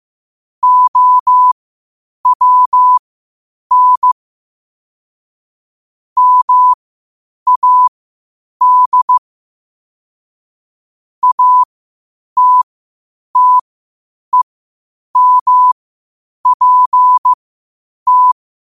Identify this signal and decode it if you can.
Morse code, heard as a keyed tone, about 15 wpm, spaced as for 8 wpm, 1000 Hz. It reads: OWN MAD ATTEMPT
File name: qotdfiles / 8wpm